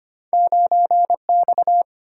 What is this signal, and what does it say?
Morse code, heard as a keyed tone, about 33 wpm, 690 Hz. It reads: 9X